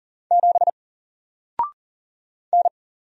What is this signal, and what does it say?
Morse code, heard as a keyed tone, about 40 wpm, 700 Hz. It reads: 7 N
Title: Single Character (Letter:Number:Punctuation:Prosign) 40wpm